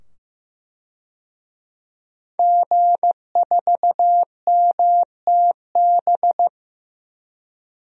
Morse code, heard as a keyed tone, about 15 wpm, 700 Hz. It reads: G4MTB
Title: DXCC-100 Call Signs - 5 Characters - 15wpm